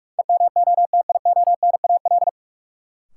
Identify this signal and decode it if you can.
Morse code, heard as a keyed tone, about 46 wpm, 700 Hz. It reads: EMOTIONAL
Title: Sentences from Top 1000 Words - Review Part 5 - New Word Drill - 45wpm